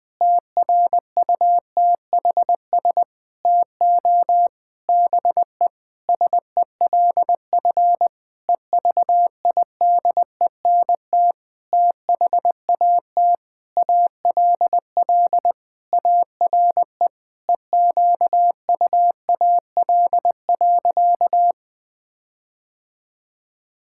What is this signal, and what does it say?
Morse code, heard as a keyed tone, about 20 wpm, 700 Hz. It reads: TRUTHS TO BE SELF EVIDENT THAT ALL ARE EQUAL.
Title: Sentences with 9-Word Phrases - 20wpm